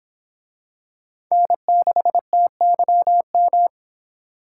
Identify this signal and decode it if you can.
Morse code, heard as a keyed tone, about 26 wpm, 700 Hz. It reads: N6TYM